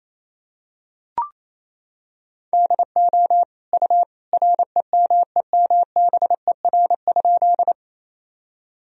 Morse code, heard as a keyed tone, about 28 wpm, 700 Hz. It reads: DO U REMEMBER?
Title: Sentences from Top 400 Words and Abbreviations 28wpm